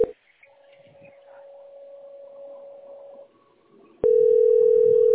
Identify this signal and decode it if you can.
Ringback tone (North America), 440 + 480 Hz.